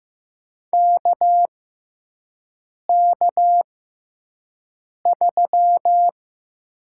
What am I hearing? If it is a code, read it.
Morse code, heard as a keyed tone, about 15 wpm, spaced as for 5 wpm, 700 Hz. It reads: KK3
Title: USA Call Signs - 6x Character Spacing - 15wpm